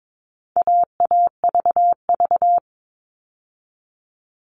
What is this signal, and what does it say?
Morse code, heard as a keyed tone, about 22 wpm, 700 Hz. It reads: AAVV
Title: All Cut Numbers - 4-Digit 22wpm